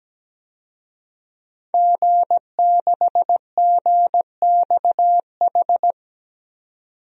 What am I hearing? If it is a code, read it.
Morse code, heard as a keyed tone, about 17 wpm, 700 Hz. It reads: G6GXH
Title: DXCC-100 Call Signs - 5 Characters - 17wpm